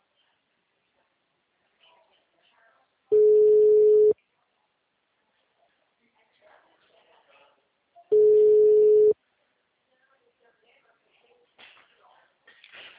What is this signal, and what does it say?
Ringback tone (Europe, 425 Hz), with speech.
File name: out- -2010- - - .0